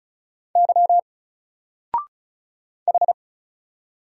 Morse code, heard as a keyed tone, about 35 wpm, 700 Hz. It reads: Y H
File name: Single-Letter-Number-35wpm